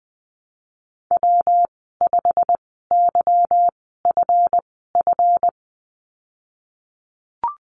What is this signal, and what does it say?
Morse code, heard as a keyed tone, about 20 wpm, spaced as for 14 wpm, 700 Hz. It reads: W5YFF
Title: 193 Lesson 31g - Letter v - Call Signs 20wpm